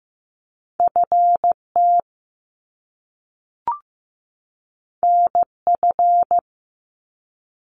Morse code, heard as a keyed tone, about 15 wpm, 700 Hz. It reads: FT NF
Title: Top Two Letter Combinations in Words 15wpm